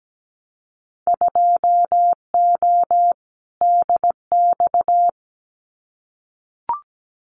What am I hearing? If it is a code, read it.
Morse code, heard as a keyed tone, about 17 wpm, 700 Hz. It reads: 2O DX
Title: Winter Field Day Exchanges Encore 17wpm